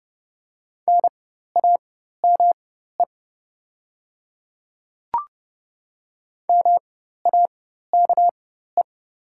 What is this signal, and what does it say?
Morse code, heard as a keyed tone, about 29 wpm, spaced as for 14 wpm, 700 Hz. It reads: NAME MAKE
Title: 4x Character Spacing - Top 300 Words - 30wpm